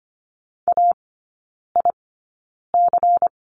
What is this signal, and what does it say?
Morse code, heard as a keyed tone, about 25 wpm, spaced as for 9 wpm, 700 Hz. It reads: AIC
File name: course / 25:10wpm / 093 Lesson 16e - Letter c - Three Characters 25wpm